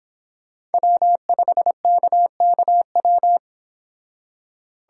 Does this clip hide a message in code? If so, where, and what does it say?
Yes: Morse code from 0.7 to 3.4 s, heard as a keyed tone, about 30 wpm, 700 Hz. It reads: W5KKW